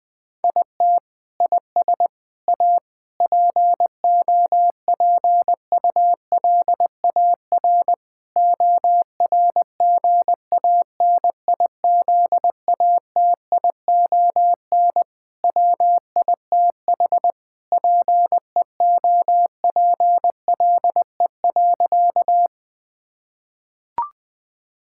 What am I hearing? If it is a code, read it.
Morse code, heard as a keyed tone, about 20 wpm, 700 Hz. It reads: IT IS A POPULAR ORGANIZATION WITH PEOPLE.
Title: Sentences from Top 1000 Words - Review Part 5 - 20wpm